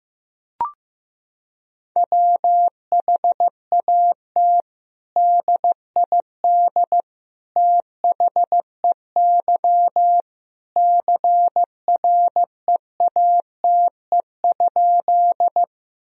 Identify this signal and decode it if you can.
Morse code, heard as a keyed tone, about 15 wpm, 700 Hz. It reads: WHAT DID THEY CREATE?
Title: Sentences from Top 1000 Words - Review Part 3 - 15wpm